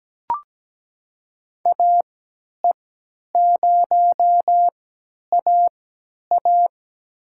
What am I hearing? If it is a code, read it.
Morse code, heard as a keyed tone, about 17 wpm, spaced as for 10 wpm, 700 Hz. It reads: AE0AA
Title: USA Call Signs - 5 Characters - 3x Character Spacing - 17wpm